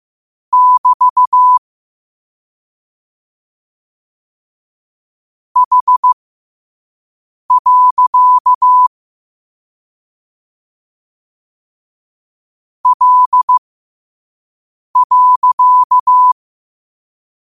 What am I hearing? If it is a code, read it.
Morse code, heard as a keyed tone, about 15 wpm, spaced as for 5 wpm, 1000 Hz. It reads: <BT> H. L.
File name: qotdfiles / 5wpm